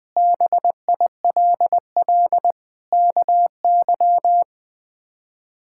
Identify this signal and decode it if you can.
Morse code, heard as a keyed tone, about 20 wpm, 700 Hz. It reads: BILL KY
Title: K1USN SST QSO Practice - 20wpm